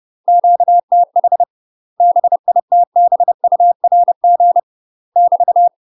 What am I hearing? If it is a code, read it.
Morse code, heard as a keyed tone, about 30 wpm, 700 Hz. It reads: QTH BITBURG <BT>
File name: UK MASTER QSOs TRAINING / 30WPM / UK MASTER QSOs 0031